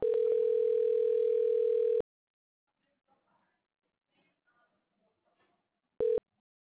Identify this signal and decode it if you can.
Ringback tone (North America), 440 + 480 Hz.